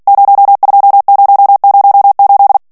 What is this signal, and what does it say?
Morse code, heard as a keyed tone, about 48 wpm, 800 Hz. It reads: 02008
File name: morse mp3 / ITALIEN texte clair 1 vitesse 45 mots minute